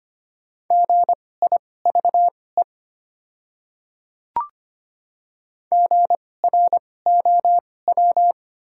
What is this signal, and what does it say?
Morse code, heard as a keyed tone, about 25 wpm, spaced as for 18 wpm, 700 Hz. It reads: GIVE GROW
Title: 2x Character Spacing - Top 300 Words - 25wpm